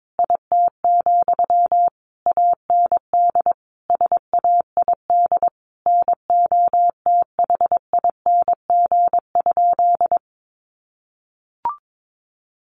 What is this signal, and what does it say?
Morse code, heard as a keyed tone, about 22 wpm, 700 Hz. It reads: IT, AND SAID NOTHING?
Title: Sentences from Top 500 Words 22wpm